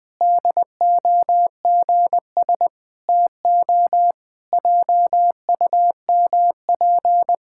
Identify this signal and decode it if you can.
Morse code, heard as a keyed tone, about 20 wpm, 700 Hz. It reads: DOGS TO JUMP
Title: Pangram Challenge v2 - 20wpm